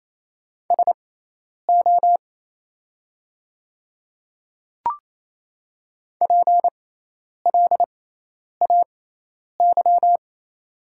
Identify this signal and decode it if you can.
Morse code, heard as a keyed tone, about 28 wpm, spaced as for 10 wpm, 700 Hz. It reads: SO PLAY
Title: 6x Character Spacing - Top 200 Words - 28wpm